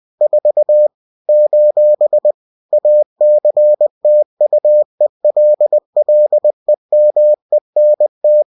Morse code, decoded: 4 : ACTUELLEMENT